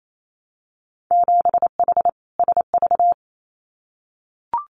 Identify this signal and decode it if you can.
Morse code, heard as a keyed tone, about 28 wpm, 700 Hz. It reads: 7H SV
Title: Winter Field Day Exchanges Encore 28wpm